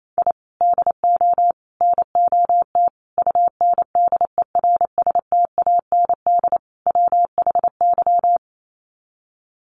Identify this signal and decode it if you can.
Morse code, heard as a keyed tone, about 28 wpm, 700 Hz. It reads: I DO NOT UNDERSTAND WHY
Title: Sets of 5 Words Encore 28wpm